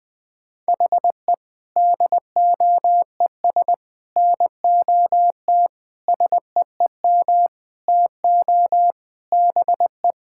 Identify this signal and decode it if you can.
Morse code, heard as a keyed tone, about 21 wpm, 700 Hz. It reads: HE DOES NOT SEEM TO BE
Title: Sentences with 5-Word Phrases - 20wpm